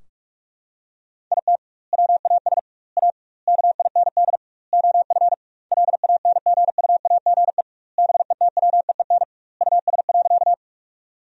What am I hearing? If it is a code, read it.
Morse code, heard as a keyed tone, about 41 wpm, 700 Hz. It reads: IT WAS A KIND OF LANGUAGE BETWEEN US.
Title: 2x Word-Spacing - Sentences from Top 500 Words - 45wpm